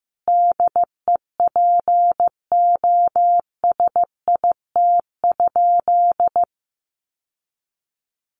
Morse code, heard as a keyed tone, about 15 wpm, 700 Hz. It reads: DEPOSIT?
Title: Sentences from Top 1300 Words 15wpm